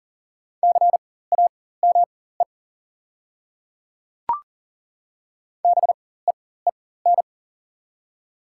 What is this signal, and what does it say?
Morse code, heard as a keyed tone, about 40 wpm, spaced as for 19 wpm, 700 Hz. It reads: CAME BEEN
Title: 4x Character Spacing - Top 200 Words - 40wpm